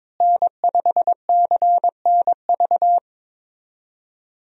Morse code, heard as a keyed tone, about 22 wpm, 700 Hz. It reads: N5CNV
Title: USA Call Signs - 5 Characters - Repeated 3x - 22wpm